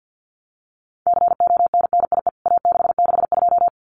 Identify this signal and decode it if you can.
Morse code, heard as a keyed tone, about 48 wpm, 700 Hz. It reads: CONNIE A662
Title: CWT Exchange 50wpm